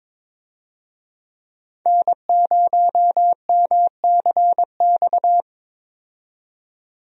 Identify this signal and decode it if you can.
Morse code, heard as a keyed tone, about 22 wpm, 700 Hz. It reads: N0MCX
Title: USA Call Signs - 5 Characters - 22wpm